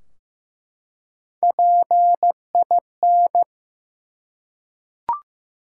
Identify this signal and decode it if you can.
Morse code, heard as a keyed tone, about 15 wpm, 700 Hz. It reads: PIN
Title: Three Letter Words - 15wpm